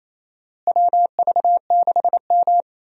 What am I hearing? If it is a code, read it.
Morse code, heard as a keyed tone, about 28 wpm, 700 Hz. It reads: WV6M
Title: USA Call Signs - 4 Characters - Repeated 3x - 28wpm